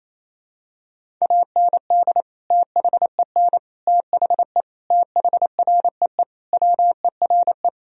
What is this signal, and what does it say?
Morse code, heard as a keyed tone, about 28 wpm, 700 Hz. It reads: AND THEN THE THREE WERE
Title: Sentences from Top 1000 Words - Review Part 1 - 28wpm